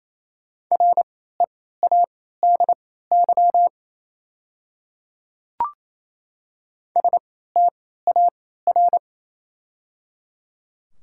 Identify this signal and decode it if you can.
Morse code, heard as a keyed tone, about 27 wpm, spaced as for 16 wpm, 700 Hz. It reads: READY STAR
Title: 3x Character Spacing - Top 500 Words - 28wpm